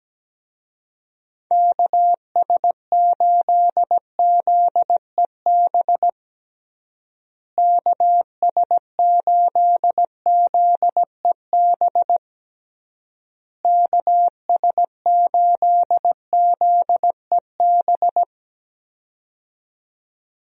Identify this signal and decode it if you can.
Morse code, heard as a keyed tone, about 17 wpm, 700 Hz. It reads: KS8ZEB KS8ZEB KS8ZEB
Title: USA Call Signs - 6 Characters - Repeated 3x - 17wpm